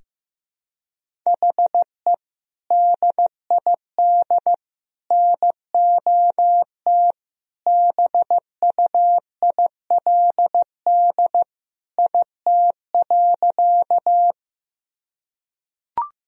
Morse code, heard as a keyed tone, about 16 wpm, 700 Hz. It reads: HE DID NOT BUILD IT.
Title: Sentences from Top 1000 Words - Review Part 1 - 15wpm